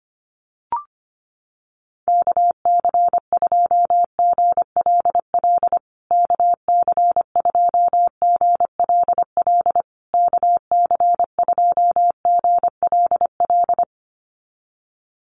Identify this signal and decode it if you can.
Morse code, heard as a keyed tone, about 25 wpm, 700 Hz. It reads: KC2GLL KC2GLL KC2GLL
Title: USA Call Signs Repeated 3x 25wpm